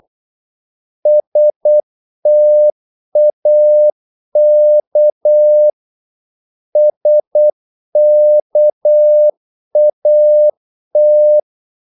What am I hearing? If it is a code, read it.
Morse code, heard as a keyed tone, about 8 wpm, 600 Hz. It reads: STAK SKAT
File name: Korte ord DK 8wpm